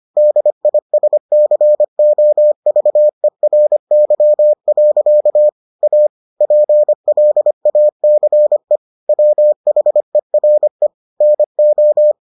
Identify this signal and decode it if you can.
Morse code, heard as a keyed tone, about 25 wpm, 600 Hz. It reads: DISCOVERY. A PLACE WHERE NO